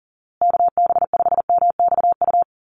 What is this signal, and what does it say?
Morse code, heard as a keyed tone, about 40 wpm, 700 Hz. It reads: KB5MXU
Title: USA Call Signs Repeated 2x 40wpm